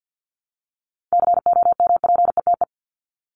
Morse code, heard as a keyed tone, about 50 wpm, 700 Hz. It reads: COMPETE
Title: Top 1751-2000 Words 50wpm